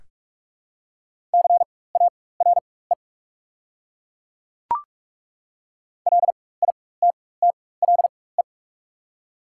Morse code, heard as a keyed tone, about 46 wpm, spaced as for 21 wpm, 700 Hz. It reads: CARE LITTLE
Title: 4x Character Spacing - Top 300 Words - 45wpm